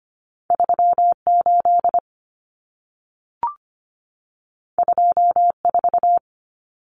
Morse code, heard as a keyed tone, about 25 wpm, 700 Hz. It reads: 38 24
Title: Numbers - 2-Digit 25wpm